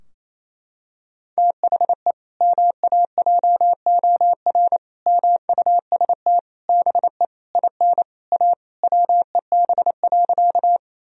Morse code, heard as a keyed tone, about 28 wpm, 700 Hz. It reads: THE MAJOR MUST BE IN A WEB.
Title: Sentences from Top 1000 Words - Review Part 4 - 28wpm